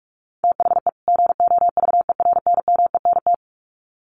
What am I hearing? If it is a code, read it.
Morse code, heard as a keyed tone, about 44 wpm, 700 Hz. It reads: THE GOVERNMENT
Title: Sets of 2 Words 45wpm